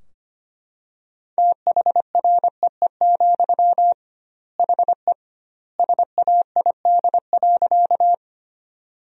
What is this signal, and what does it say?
Morse code, heard as a keyed tone, about 25 wpm, 700 Hz. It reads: THREE, HE SAID.
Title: 2x Word-Spacing - Sentences from Top 200 Words - 25wpm